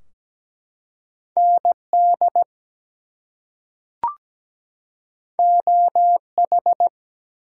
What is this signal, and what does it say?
Morse code, heard as a keyed tone, about 17 wpm, 700 Hz. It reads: ND OH
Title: US States and Canadian Provinces 17wpm